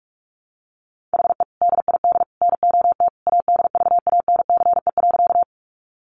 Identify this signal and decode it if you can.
Morse code, heard as a keyed tone, about 44 wpm, 700 Hz. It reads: HE DID NOT ADVANCE.
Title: Sentences from Top 1400 Words 45wpm